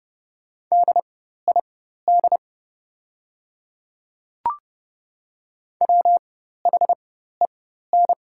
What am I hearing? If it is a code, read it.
Morse code, heard as a keyed tone, about 30 wpm, spaced as for 14 wpm, 690 Hz. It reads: DID WHEN
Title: 4x Character Spacing - Top 100 Words - 30wpm